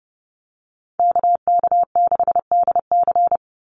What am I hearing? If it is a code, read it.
Morse code, heard as a keyed tone, about 30 wpm, 700 Hz. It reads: KK6DC
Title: USA Call Signs 30wpm